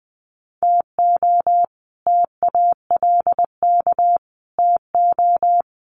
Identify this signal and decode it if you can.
Morse code, heard as a keyed tone, about 20 wpm, 700 Hz. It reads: TO TALK TO
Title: Sets of 4 Words 20wpm